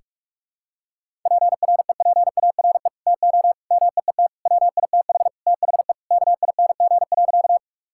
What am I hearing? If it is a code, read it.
Morse code, heard as a keyed tone, about 46 wpm, 700 Hz. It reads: PREPARE TO MEET WITH THE KING.
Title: Sentences from Top 1000 Words - Review Part 6 - 45wpm